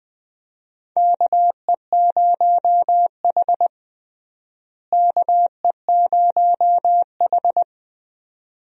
Morse code, heard as a keyed tone, about 20 wpm, 700 Hz. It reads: KE0H KE0H
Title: USA Call Signs - 4 Characters - Repeated 2x - 20wpm